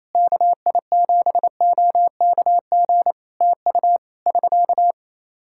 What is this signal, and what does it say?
Morse code, heard as a keyed tone, about 28 wpm, 700 Hz. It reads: KI7OKG TU <SK>
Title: K1USN SST QSO Practice 28wpm